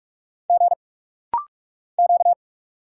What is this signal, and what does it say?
Morse code, heard as a keyed tone, about 46 wpm, 690 Hz. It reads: G Q
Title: Single Letters - Rapid-Fire 45wpm